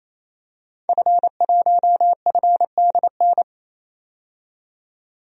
Morse code, heard as a keyed tone, about 28 wpm, 700 Hz. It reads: F1FDN
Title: DXCC-100 Call Signs - 5 Characters - 28wpm